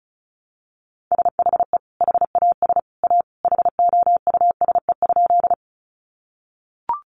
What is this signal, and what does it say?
Morse code, heard as a keyed tone, about 34 wpm, 700 Hz. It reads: SHE HAS A HOUSE?
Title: Sentences from Top 200 Words 35wpm